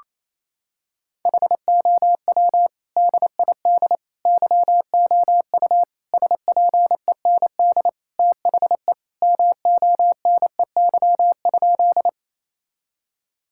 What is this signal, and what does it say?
Morse code, heard as a keyed tone, about 28 wpm, 700 Hz. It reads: HOW DID YOU SPEND THE MONEY?
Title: Sentences from Top 1000 Words - Review Part 5 - 28wpm